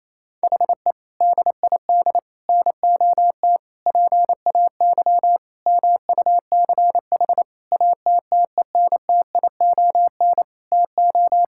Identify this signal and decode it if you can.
Morse code, heard as a keyed tone, about 27 wpm, 700 Hz. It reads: HE DID NOT PAY MUCH ATTENTION TO
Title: Sentences from Top 1000 Words - Review Part 6 - 28wpm